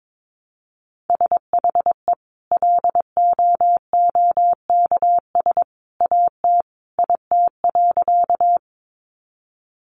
Morse code, heard as a keyed tone, about 22 wpm, 700 Hz. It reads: SHE LOOKS AT IT.
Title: Additional Easy Phrases 22wpm